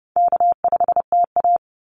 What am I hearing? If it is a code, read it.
Morse code, heard as a keyed tone, about 30 wpm, 700 Hz. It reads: K5TA
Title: Call Signs Repeated 2x - 30wpm